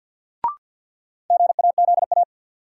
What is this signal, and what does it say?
Morse code, heard as a keyed tone, about 53 wpm, 700 Hz. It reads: GAZA
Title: Sets of 5 Words - New Word Drill - 50wpm